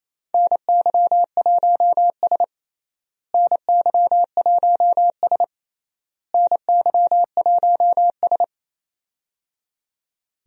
Morse code, heard as a keyed tone, about 27 wpm, 700 Hz. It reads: NY1S NY1S NY1S
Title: USA Call Signs - 4 Characters - Repeated 3x - 28wpm